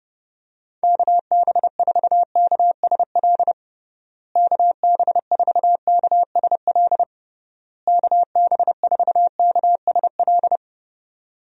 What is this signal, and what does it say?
Morse code, heard as a keyed tone, about 30 wpm, 700 Hz. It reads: KB4KSL KB4KSL KB4KSL
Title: USA Call Signs - 6 Characters - Repeated 3x - 30wpm